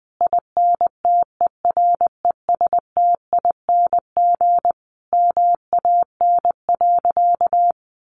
Morse code, heard as a keyed tone, about 20 wpm, 700 Hz. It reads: INTERESTING MAN.
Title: Sentences from Top 1000 Words 20wpm